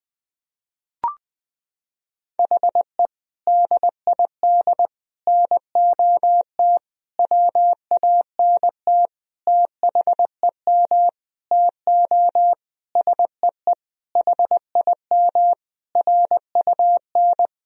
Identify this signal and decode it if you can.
Morse code, heard as a keyed tone, about 21 wpm, 700 Hz. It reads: HE DID NOT WANT THEM TO SEE HIM RUN
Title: Sentences from Top 1000 Words - Review Part 1 - 20wpm